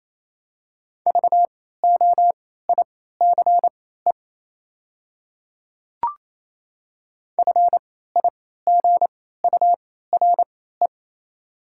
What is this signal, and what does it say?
Morse code, heard as a keyed tone, about 28 wpm, spaced as for 16 wpm, 700 Hz. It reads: VOICE FIGURE
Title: 3x Character Spacing - Top 500 Words - 28wpm